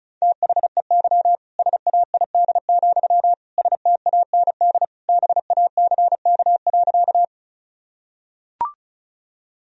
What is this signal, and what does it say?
Morse code, heard as a keyed tone, about 36 wpm, 700 Hz. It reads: THEY SAID, STAND BACK.
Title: Sentences from Top 1000 Words - Review Part 1 - 35wpm